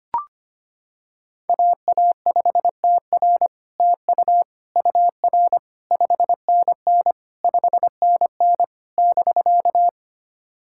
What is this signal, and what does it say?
Morse code, heard as a keyed tone, about 25 wpm, 700 Hz. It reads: AA5TR TU UR 5NN 5NN <BK>
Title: SOTA QSO Practice 25wpm